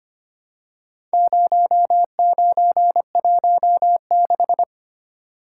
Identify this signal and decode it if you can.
Morse code, heard as a keyed tone, about 25 wpm, 700 Hz. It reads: 0916
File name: practice / 25wpm / Four Letter-Number Combinations - 25wpm